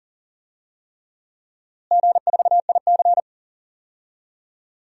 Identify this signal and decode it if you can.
Morse code, heard as a keyed tone, about 40 wpm, 700 Hz. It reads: G4IC
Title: DXCC-100 Call Signs - 4 Characters - Repeated 3x - 40wpm